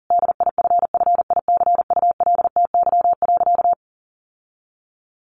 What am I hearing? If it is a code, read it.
Morse code, heard as a keyed tone, about 40 wpm, 700 Hz. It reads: DIFFICULTY.
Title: Sentences from Top 1500 Words 40wpm